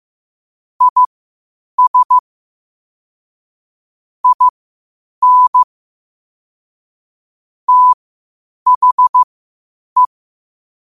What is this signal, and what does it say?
Morse code, heard as a keyed tone, about 15 wpm, spaced as for 8 wpm, 1000 Hz. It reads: IS IN THE
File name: qotdfiles / 8wpm